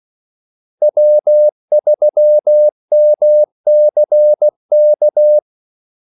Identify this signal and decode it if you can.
Morse code, heard as a keyed tone, about 16 wpm, 600 Hz. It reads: W3MCK
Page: Kaldesignaler 16 wpm